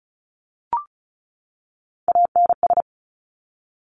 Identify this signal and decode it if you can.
Morse code, heard as a keyed tone, about 36 wpm, 700 Hz. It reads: ANS
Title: Top Three Letter Combinations in Words 35wpm